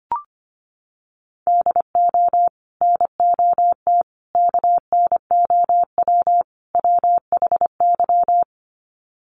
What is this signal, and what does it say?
Morse code, heard as a keyed tone, about 25 wpm, 700 Hz. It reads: DO NOT KNOW WHY
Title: Sets of 4 Words 25wpm